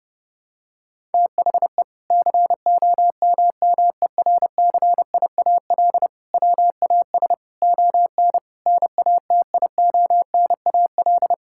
Morse code, heard as a keyed tone, about 30 wpm, 700 Hz. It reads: THE COMMERCIAL WAS ON NATIONAL
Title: Sentences from Top 1000 Words - Review Part 5 - 30wpm